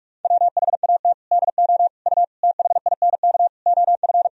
Morse code, heard as a keyed tone, about 46 wpm, 700 Hz. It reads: WHAT DO U THINK OF
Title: Sentences from Top 200 Words and Abbreviations 45wpm